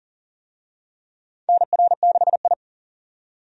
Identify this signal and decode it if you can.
Morse code, heard as a keyed tone, about 40 wpm, 700 Hz. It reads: NR6I
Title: USA Call Signs - 4 Characters - 40wpm